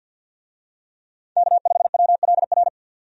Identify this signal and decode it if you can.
Morse code, heard as a keyed tone, about 51 wpm, 700 Hz. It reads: K5WLR
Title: USA Call Signs - 5 Characters - 50wpm